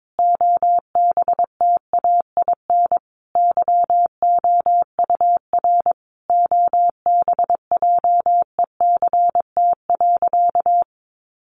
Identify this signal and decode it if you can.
Morse code, heard as a keyed tone, about 22 wpm, 700 Hz. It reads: OBTAIN YOUR OBJECT.
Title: Sentences from Top 1300 Words 22wpm